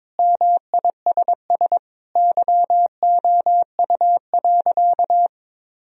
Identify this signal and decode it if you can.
Morse code, heard as a keyed tone, about 22 wpm, 700 Hz. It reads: MISS YOU.
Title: Sentences from Top 1000 Words - Review Part 3 - 22wpm